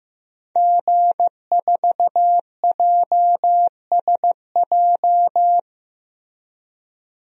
Morse code, heard as a keyed tone, about 15 wpm, 700 Hz. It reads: G4JSJ
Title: DXCC-100 Call Signs - 5 Characters - Repeated 2x - 15wpm